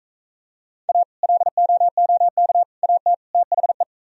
Morse code, heard as a keyed tone, about 42 wpm, 700 Hz. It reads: A LOOK AT THE